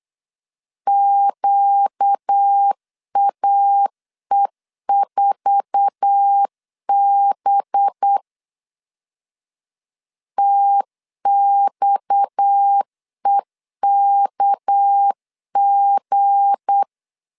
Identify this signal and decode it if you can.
Morse code, heard as a keyed tone, about 8 wpm, 800 Hz. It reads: QAE4B TXEKG
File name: LEZIONI / 9 28 8wpm 3512